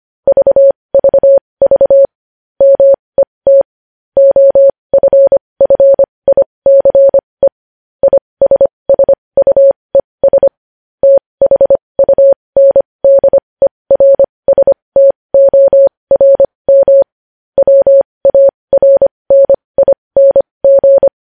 Morse code, heard as a keyed tone, about 25 wpm, spaced as for 20 wpm, 550 Hz. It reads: VVV MET OFFICE ISSUES THUNDERSTORM WARNING
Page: News Headlines in Morse Code at 20 WPM